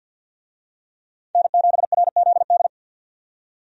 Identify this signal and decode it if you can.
Morse code, heard as a keyed tone, about 51 wpm, 700 Hz. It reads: N7RZD
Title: USA Call Signs - 5 Characters - 50wpm